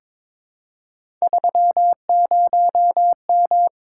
Morse code, heard as a keyed tone, about 22 wpm, 700 Hz. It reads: 30M
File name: practice / 22wpm / Three Letter-Number Combinations - 22wpm